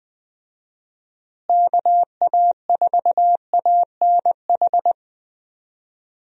Morse code, heard as a keyed tone, about 21 wpm, 700 Hz. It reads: KA4ANH
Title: USA Call Signs - 6 Characters - Repeated 2x - 20wpm